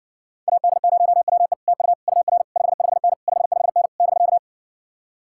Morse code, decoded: AD0WE TU UR 55N 55N <BK>

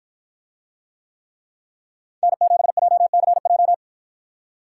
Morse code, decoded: N7JCJ

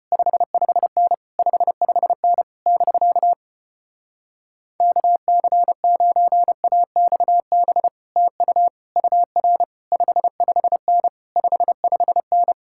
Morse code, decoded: 55N 55N <BK> KC9AXB TU UR 55N 55N

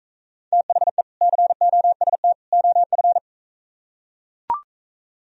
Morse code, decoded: THE COST OF

